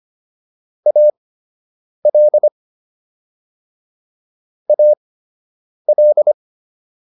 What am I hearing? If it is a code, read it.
Morse code, heard as a keyed tone, about 25 wpm, spaced as for 8 wpm, 600 Hz. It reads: AL AL